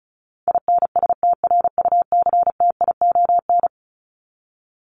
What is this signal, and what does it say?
Morse code, heard as a keyed tone, about 34 wpm, 700 Hz. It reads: INSTRUCTION
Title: Top 1751-2000 Words 35wpm